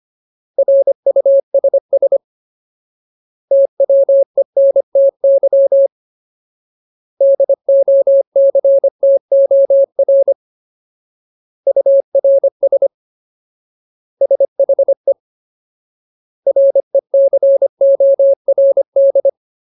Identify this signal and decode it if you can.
Morse code, heard as a keyed tone, about 25 wpm, 550 Hz. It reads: RUSS TWENTY DOCTOR URS SHE RECORD
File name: Ultimate Word List / 25WPM Ultimate